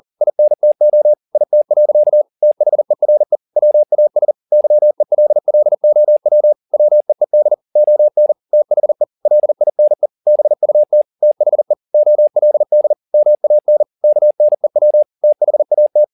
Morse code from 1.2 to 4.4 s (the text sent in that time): IT. THERE WAS